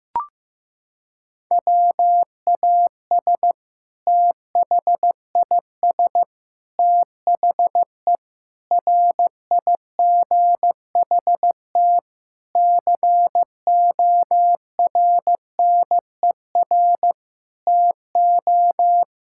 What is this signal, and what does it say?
Morse code, heard as a keyed tone, about 16 wpm, 700 Hz. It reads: WAS THIS THE RIGHT CORNER TO
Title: Sentences from Top 1600 Words 15wpm